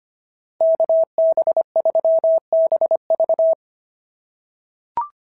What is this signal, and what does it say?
Morse code, heard as a keyed tone, about 24 wpm, 650 Hz. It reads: KB3BV
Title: SOTA QSO Practice 25wpm